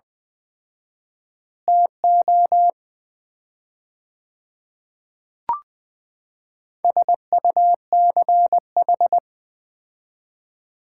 Morse code, heard as a keyed tone, about 20 wpm, 700 Hz. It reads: TO SUCH
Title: 4x Word-Spacing - Sets of 3 Words - 20wpm